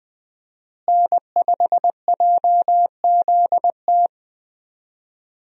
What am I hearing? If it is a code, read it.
Morse code, heard as a keyed tone, about 20 wpm, 700 Hz. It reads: N5JZT
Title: USA Call Signs - 5 Characters - Repeated 2x - 20wpm